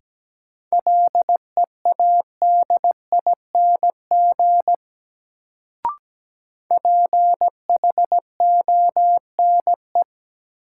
Morse code, heard as a keyed tone, about 17 wpm, 700 Hz. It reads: LEADING PHONE
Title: Sentences from Top 1100 Words - New Word Drill - 17wpm